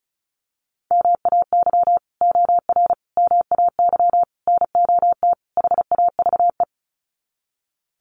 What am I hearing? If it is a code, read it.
Morse code, heard as a keyed tone, about 34 wpm, 700 Hz. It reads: MAY OR MAY NOT HAVE
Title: Sets of 5 Words Encore2 35wpm